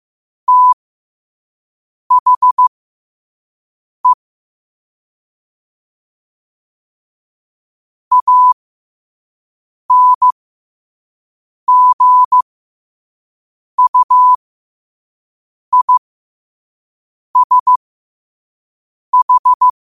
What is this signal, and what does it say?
Morse code, heard as a keyed tone, about 15 wpm, spaced as for 5 wpm, 1000 Hz. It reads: THE ANGUISH
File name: qotdfiles / 5wpm